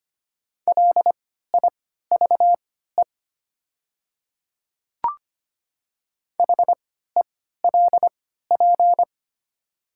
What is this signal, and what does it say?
Morse code, heard as a keyed tone, about 25 wpm, spaced as for 14 wpm, 700 Hz. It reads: LIVE HELP
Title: 3x Character Spacing - Top 200 Words - 25wpm